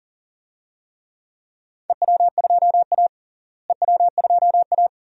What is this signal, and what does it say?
Morse code, heard as a keyed tone, about 40 wpm, 700 Hz. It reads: EW2A EW2A
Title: DXCC-100 Call Signs - 4 Characters - Repeated 2x - 40wpm